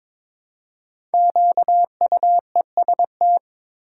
Morse code, heard as a keyed tone, about 23 wpm, 700 Hz. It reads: QUEST
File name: 5 Letter Words - First 3 Letters Given - 22wpm